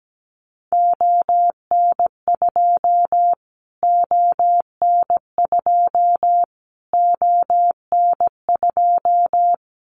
Morse code, heard as a keyed tone, about 17 wpm, 700 Hz. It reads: ON2 ON2 ON2
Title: Callsign Prefixes for Top 25 DXCC - Repeated 3x - 17wpm